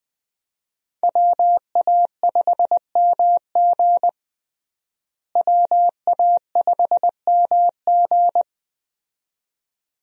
Morse code, heard as a keyed tone, about 20 wpm, 700 Hz. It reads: WA5MG WA5MG
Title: USA Call Signs - 5 Characters - Repeated 2x - 20wpm